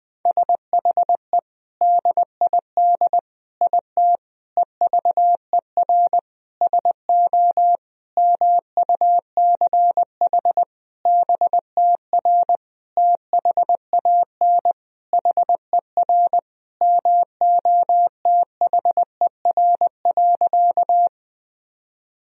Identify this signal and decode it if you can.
Morse code, heard as a keyed tone, about 20 wpm, 700 Hz. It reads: SHE DID IT EVER SO MUCH BTR THAN HER MOTHER.
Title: Sentences from Top 400 Words and Abbreviations 20wpm